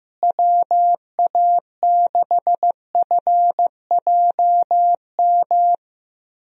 Morse code, heard as a keyed tone, about 15 wpm, 700 Hz. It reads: WA6FJM
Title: USA Call Signs - 6 Characters - Repeated 2x - 15wpm